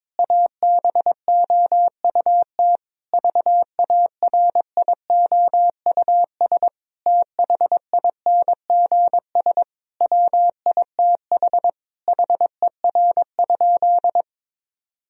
Morse code, heard as a keyed tone, about 22 wpm, 700 Hz. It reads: ABOUT VARIOUS THINGS WITH HER?
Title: Sentences from Top 1000 Words - Review Part 4 - 22wpm